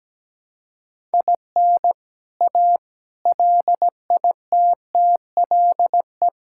Morse code, heard as a keyed tone, about 17 wpm, 700 Hz. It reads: IN A LITTLE
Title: Sentences from Top 1000 Words - Review Part 5 - 17wpm